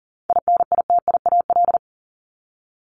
Morse code, heard as a keyed tone, about 40 wpm, 700 Hz. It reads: INITIAL
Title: Top 1501-1750 Words 40wpm